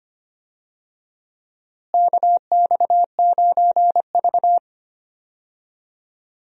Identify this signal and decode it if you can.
Morse code, heard as a keyed tone, about 25 wpm, 700 Hz. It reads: KX9V
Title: USA Call Signs - 4 Characters - Repeated 2x - 25wpm